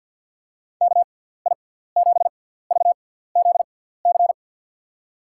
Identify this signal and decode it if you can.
Morse code, heard as a keyed tone, about 50 wpm, 700 Hz. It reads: K I 7 V Z C
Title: USA Call Signs - 6x Character Spacing - 50wpm